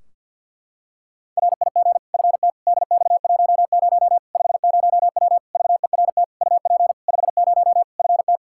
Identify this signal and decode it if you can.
Morse code, heard as a keyed tone, about 51 wpm, 700 Hz. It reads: RIG FT DX10 50W VERT UP 50 FT
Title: Practice QSO - Exchange 2 - Info - 50wpm